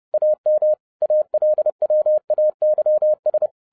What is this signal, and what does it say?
Morse code, heard as a keyed tone, about 30 wpm, 610 Hz. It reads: AM ALWAYS
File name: 2011-07-mp3 / n40 EXCITING QRP CW OPERATION 30wpm